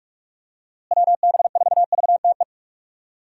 Morse code, heard as a keyed tone, about 46 wpm, 700 Hz. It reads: WB4VTE
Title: USA Call Signs - 6 Characters - Repeated 3x - 45wpm